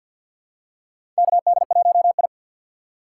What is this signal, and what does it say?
Morse code, heard as a keyed tone, about 48 wpm, 700 Hz. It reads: KD1I
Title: USA Call Signs - 4 Characters - 2x Character Spacing - 50wpm